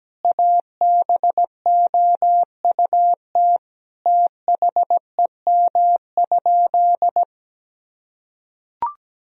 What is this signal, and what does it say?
Morse code, heard as a keyed tone, about 17 wpm, 700 Hz. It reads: ABOUT THEM?
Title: Sentences from Top 1000 Words - Review Part 3 - 17wpm